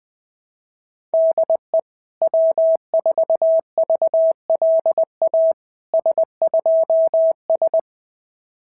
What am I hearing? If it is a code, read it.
Morse code, heard as a keyed tone, about 19 wpm, 650 Hz. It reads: DE W4VLA S2S
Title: SOTA QSO Practice 20wpm